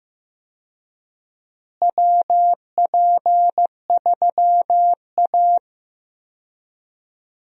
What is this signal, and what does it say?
Morse code, heard as a keyed tone, about 15 wpm, 700 Hz. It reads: WP3A
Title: USA Call Signs - 4 Characters - Repeated 2x - 15wpm